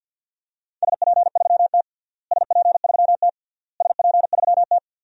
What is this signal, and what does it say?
Morse code, heard as a keyed tone, about 52 wpm, 700 Hz. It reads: SP3T SP3T SP3T
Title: DXCC-100 Call Signs - 4 Characters - Repeated 3x - 50wpm